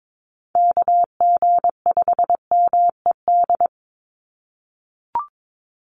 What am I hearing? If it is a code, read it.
Morse code, heard as a keyed tone, about 22 wpm, 700 Hz. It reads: KG5MED
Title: USA Call Signs 22wpm